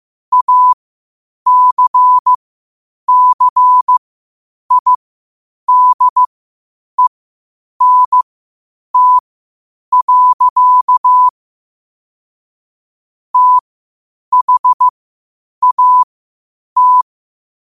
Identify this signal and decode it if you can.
Morse code, heard as a keyed tone, about 15 wpm, spaced as for 8 wpm, 1000 Hz. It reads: ACCIDENT. THAT
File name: qotdfiles / 8wpm